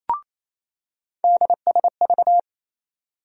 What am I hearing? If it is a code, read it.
Morse code, heard as a keyed tone, about 28 wpm, 700 Hz. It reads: DSV
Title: USA Call Sign Postfixes - 3 Letters 28wpm